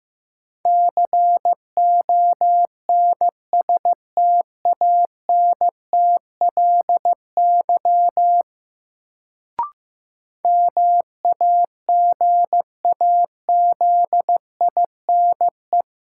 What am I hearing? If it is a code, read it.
Morse code, heard as a keyed tone, about 16 wpm, 700 Hz. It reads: CONSTANTLY MAGAZINE
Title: Sentences from Top 1000 Words - Review Part 7 - New Word Drill - 15wpm